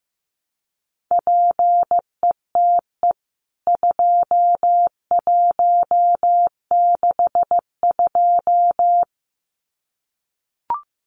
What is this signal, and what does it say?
Morse code, heard as a keyed tone, about 15 wpm, 700 Hz. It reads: PETE 2162
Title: CWT Exchange 15wpm